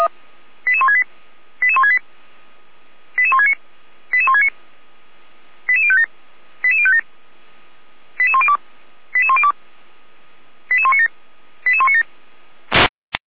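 Five-tone selective call (ZVEI), heard as quick runs of five tones, tones of about 70 ms. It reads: 88168 88168 88169 88169 88057 88057 88102 88102 88197 88197
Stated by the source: ZVEI five-tone selective call received by SDR.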